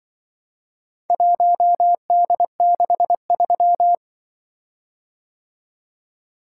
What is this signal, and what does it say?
Morse code, heard as a keyed tone, about 25 wpm, 700 Hz. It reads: 1D63
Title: Speed-Racing - Four Letter-Number Combinations - 20wpm